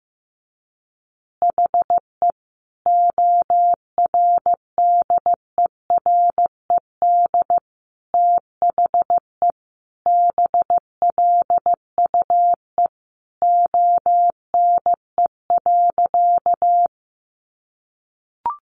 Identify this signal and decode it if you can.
Morse code, heard as a keyed tone, about 15 wpm, 700 Hz. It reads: HE ORDERED THE BLUE ONE.
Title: Additional Easy Phrases 15wpm